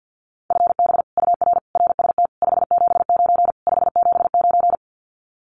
Morse code, heard as a keyed tone, about 50 wpm, 700 Hz. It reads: FB UR RST 579 579
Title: Common QSO Phrases Encore 50wpm